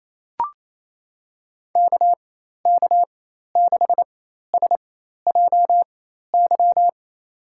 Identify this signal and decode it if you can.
Morse code, heard as a keyed tone, about 28 wpm, spaced as for 13 wpm, 700 Hz. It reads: KK6SJY
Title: USA Call Signs - 6 Characters - 4x Character Spacing - 28wpm